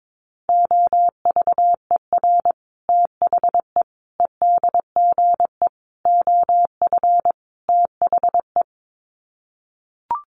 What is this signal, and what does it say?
Morse code, heard as a keyed tone, about 22 wpm, 700 Hz. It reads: OVER THE EDGE OF THE
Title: Sets of 5 Words Encore2 22wpm